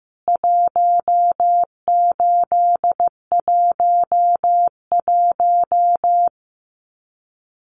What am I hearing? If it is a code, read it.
Morse code, heard as a keyed tone, about 15 wpm, 700 Hz. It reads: 1811
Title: CWT Exchange 15wpm